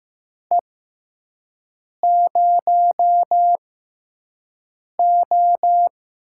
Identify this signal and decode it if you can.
Morse code, heard as a keyed tone, about 15 wpm, spaced as for 5 wpm, 700 Hz. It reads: E0O
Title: USA Call Signs - 4 Characters - 6x Character Spacing - 15wpm